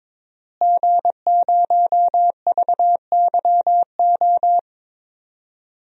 Morse code, heard as a keyed tone, about 22 wpm, 700 Hz. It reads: G0VYO
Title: DXCC-100 Call Signs - 5 Characters - Repeated 3x - 22wpm